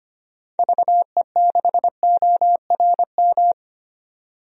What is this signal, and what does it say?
Morse code, heard as a keyed tone, about 25 wpm, 700 Hz. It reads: VE6ORM
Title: DXCC-100 Call Signs - 6 Characters - 25wpm